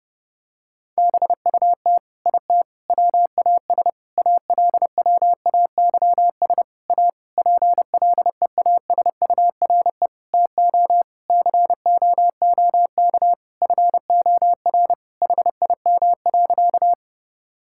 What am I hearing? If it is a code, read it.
Morse code, heard as a keyed tone, about 30 wpm, 700 Hz. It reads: BUT IT WAS ALWAYS A PLEASURE TO COOK FOR HIM.
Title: Sentences from Top 1600 Words 30wpm